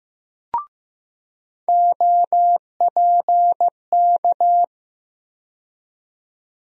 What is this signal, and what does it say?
Morse code, heard as a keyed tone, about 15 wpm, 700 Hz. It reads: OPK
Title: USA Call Sign Postfixes - 3 Letters 15wpm